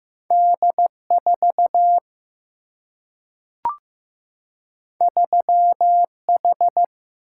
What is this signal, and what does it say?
Morse code, heard as a keyed tone, about 15 wpm, 700 Hz. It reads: D 4 3 H
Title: Single-Letter Single-Number 15wpm